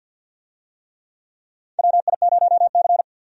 Morse code, heard as a keyed tone, about 52 wpm, 700 Hz. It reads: WI0C
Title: USA Call Signs - 4 Characters - 50wpm